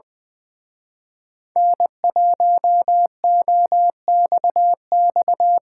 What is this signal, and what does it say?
Morse code, heard as a keyed tone, about 22 wpm, 700 Hz. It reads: N1OXX
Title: Speed Racing - USA 1x3 Call Signs 20wpm